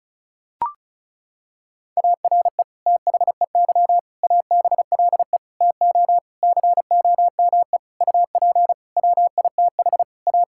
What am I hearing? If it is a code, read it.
Morse code, heard as a keyed tone, about 36 wpm, 690 Hz. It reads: ARE THEY ABLE TO COME UP WITH A
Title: Sentences from Top 1000 Words - Review Part 5 - 35wpm